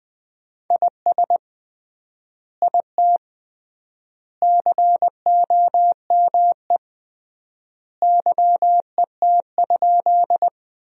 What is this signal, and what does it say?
Morse code, heard as a keyed tone, about 20 wpm, 700 Hz. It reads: IS IT COME YET?
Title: 3x Word-Spacing - Sentences from Top 500 Words - 20wpm